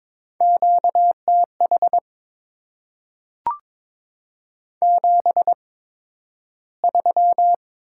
Morse code, heard as a keyed tone, about 22 wpm, spaced as for 12 wpm, 700 Hz. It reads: QTH 73
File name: Top 20 QSO Elements - 8x Character Spacing - 22wpm